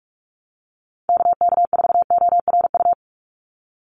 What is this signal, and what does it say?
Morse code, heard as a keyed tone, about 44 wpm, 700 Hz. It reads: KK4ORU
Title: USA Call Signs 45wpm